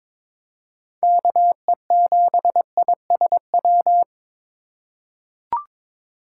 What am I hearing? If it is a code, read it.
Morse code, heard as a keyed tone, about 22 wpm, 700 Hz. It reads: KE7ISW
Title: USA Call Signs - 6 Characters - 4x Character Spacing - 22wpm